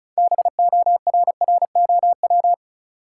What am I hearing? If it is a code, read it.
Morse code, heard as a keyed tone, about 36 wpm, 700 Hz. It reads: BORROW
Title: Sentences from Top 1500 Words - New Word Drill - 35wpm